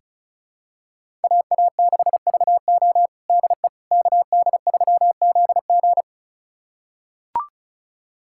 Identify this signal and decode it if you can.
Morse code, heard as a keyed tone, about 34 wpm, 700 Hz. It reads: AA6VO DE KD3ZG
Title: USA Callsign DE Callsign - 5 Characters - 35wpm